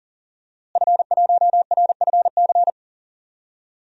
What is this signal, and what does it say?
Morse code, heard as a keyed tone, about 40 wpm, 700 Hz. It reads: F1RFC
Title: DXCC-100 Call Signs - 5 Characters - Repeated 3x - 40wpm